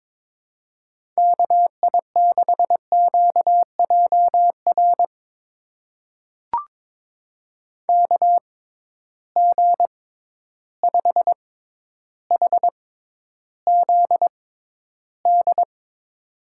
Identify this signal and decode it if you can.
Morse code, heard as a keyed tone, about 22 wpm, spaced as for 11 wpm, 700 Hz. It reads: KI6QJR KG5HZD
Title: USA Call Signs - 6 Characters - 6x Character Spacing - 22wpm